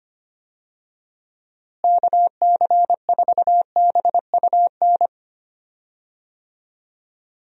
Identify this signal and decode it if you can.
Morse code, heard as a keyed tone, about 25 wpm, 700 Hz. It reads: KC4BUN
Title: USA Call Signs - 6 Characters - Repeated 2x - 25wpm